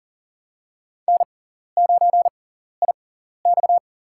Morse code, heard as a keyed tone, about 40 wpm, spaced as for 14 wpm, 700 Hz. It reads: N9IX
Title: USA Call Signs - 4 Characters - 6x Character Spacing - 40wpm